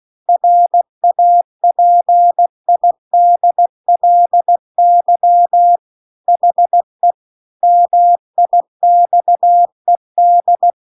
Morse code, heard as a keyed tone, about 16 wpm, 700 Hz. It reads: RAPIDLY HE MIXED
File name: CW-The-Big-Twoheaded-River-2-16WPM